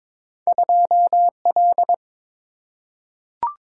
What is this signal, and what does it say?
Morse code, heard as a keyed tone, about 22 wpm, 700 Hz. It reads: 2L
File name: practice / 22wpm / Two Letter-Number Combinations - 22wpm